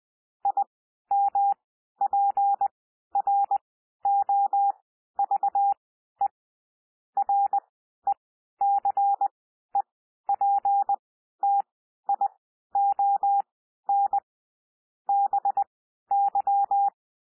Morse code, heard as a keyed tone, about 20 wpm, spaced as for 12 wpm, 800 Hz. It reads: IMPROVE RECEPTION BY
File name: cwmp3 / P1 SOUND PASSAGE 20 wpm